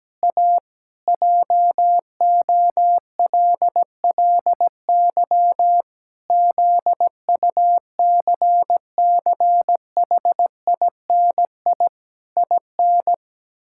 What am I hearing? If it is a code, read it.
Morse code, heard as a keyed tone, about 17 wpm, 700 Hz. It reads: A JOLLY ZUCCHINI IN
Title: Pangram Challenge v2 - 17wpm